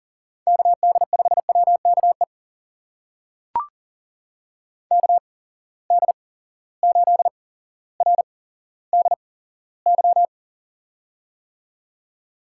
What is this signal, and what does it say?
Morse code, heard as a keyed tone, about 40 wpm, spaced as for 17 wpm, 700 Hz. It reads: KD5WKE KD8RDY
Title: USA Call Signs - 6 Characters - 8x Character Spacing - 40wpm